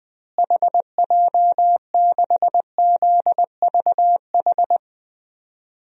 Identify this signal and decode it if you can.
Morse code, heard as a keyed tone, about 20 wpm, 700 Hz. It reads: HJ6ZVH
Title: DXCC-100 Call Signs - 6 Characters - Repeated 3x - 20wpm